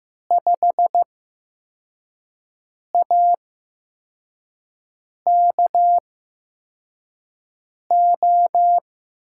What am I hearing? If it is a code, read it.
Morse code, heard as a keyed tone, about 15 wpm, spaced as for 4 wpm, 700 Hz. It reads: 5AKO
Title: USA Call Signs - 5 Characters - 8x Character Spacing - 15wpm